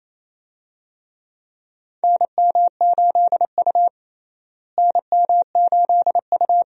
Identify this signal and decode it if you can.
Morse code, heard as a keyed tone, about 28 wpm, 700 Hz. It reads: NM8U NM8U
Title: USA Call Signs - 4 Characters - Repeated 2x - 28wpm